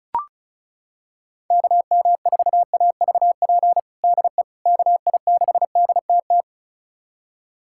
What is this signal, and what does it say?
Morse code, heard as a keyed tone, about 34 wpm, 700 Hz. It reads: KM4AVP DE KI6DTT
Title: USA Callsign DE Callsign - 6 Characters - 35wpm